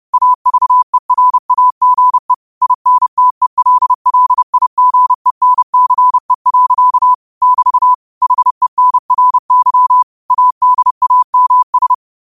Morse code, heard as a keyed tone, about 30 wpm, 1000 Hz. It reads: AVERAGE INTELLIGENCE. <BT> HENRY ADAMS
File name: qotdfiles / 30wpm